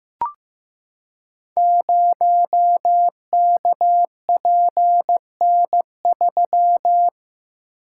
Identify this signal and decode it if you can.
Morse code, heard as a keyed tone, about 15 wpm, 700 Hz. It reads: 0KPN3
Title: Code Groups - 5 Letter:Numbers - 15wpm